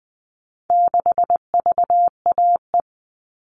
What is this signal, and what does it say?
Morse code, heard as a keyed tone, about 20 wpm, 700 Hz. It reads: 6VAE
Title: All Cut Numbers - 4-Digit 20wpm